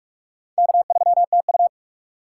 Morse code, heard as a keyed tone, about 46 wpm, 700 Hz. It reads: K3TU
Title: USA Call Signs - 4 Characters - 2x Character Spacing - 45wpm